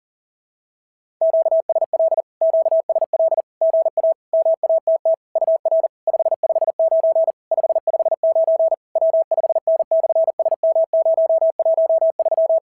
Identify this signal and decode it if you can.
Morse code, heard as a keyed tone, about 40 wpm, 690 Hz. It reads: QSL QSL GA MATT UR 559 559 W5N/SM013
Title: SOTA QSO Practice 40wpm